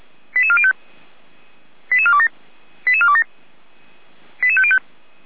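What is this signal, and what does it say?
Five-tone selective call (ZVEI), heard as quick runs of five tones, tones of about 70 ms. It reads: 88405 88427 88427 88505
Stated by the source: ZVEI five-tone selective call received on an SDR.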